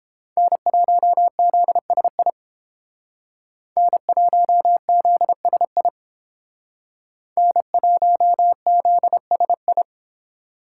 Morse code, heard as a keyed tone, about 29 wpm, 700 Hz. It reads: N1ZSI N1ZSI N1ZSI